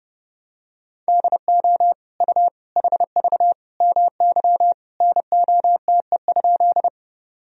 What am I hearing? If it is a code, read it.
Morse code, heard as a keyed tone, about 30 wpm, 700 Hz. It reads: DO U HV MY NOTE?
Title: Sentences from Top 500 Words and Abbreviations 30wpm